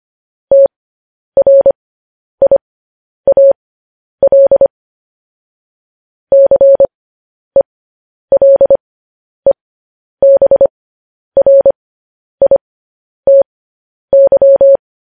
Morse code, heard as a keyed tone, about 25 wpm, spaced as for 10 wpm, 550 Hz. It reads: TRIAL CELEBRITY
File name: podcast / News headlines-1x-25x10